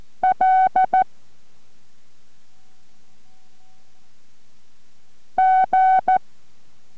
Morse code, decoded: L G